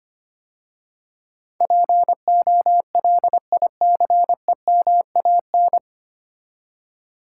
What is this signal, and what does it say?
Morse code, heard as a keyed tone, about 25 wpm, 700 Hz. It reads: POLICEMAN
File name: practice / 25wpm / 9 Letter Words - First 3 Letters Given - 25wpm